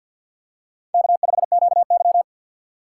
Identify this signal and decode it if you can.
Morse code, heard as a keyed tone, about 46 wpm, 700 Hz. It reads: K5QY
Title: USA Call Signs - 4 Characters - 3x Character Spacing - 50wpm